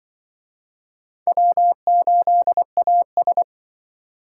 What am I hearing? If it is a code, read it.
Morse code, heard as a keyed tone, about 25 wpm, 700 Hz. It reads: W8AS
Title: Speed Racing - USA 1x2 Call Signs 20wpm